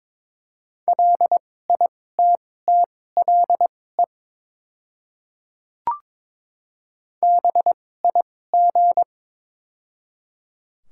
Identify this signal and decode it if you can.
Morse code, heard as a keyed tone, about 22 wpm, spaced as for 16 wpm, 700 Hz. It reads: LITTLE BIG
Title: 2x Character Spacing - Top 200 Words - 22wpm